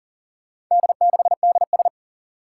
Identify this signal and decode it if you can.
Morse code, heard as a keyed tone, about 40 wpm, 700 Hz. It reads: D6DS
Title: Four Challenging Letter-Number Combinations 40wpm